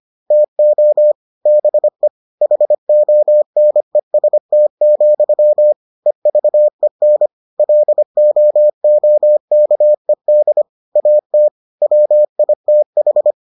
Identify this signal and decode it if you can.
Morse code, heard as a keyed tone, about 25 wpm, 600 Hz. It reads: TO BE HONEST, EVEN LOOKED AT WITH